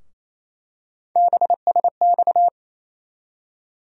Morse code, heard as a keyed tone, about 28 wpm, 700 Hz. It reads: BSX
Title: USA Call Sign Postfixes - 3 Letters 28wpm